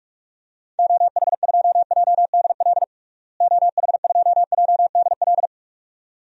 Morse code, heard as a keyed tone, about 46 wpm, 700 Hz. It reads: OH2JDL OH2JDL
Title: DXCC-100 Call Signs - 6 Characters - Repeated 2x - 45wpm